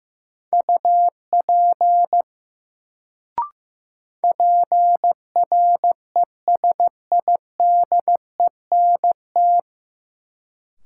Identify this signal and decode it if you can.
Morse code, heard as a keyed tone, about 15 wpm, 700 Hz. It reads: UP PRESIDENT
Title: Sets of 2 Words - New Word Drill - 15wpm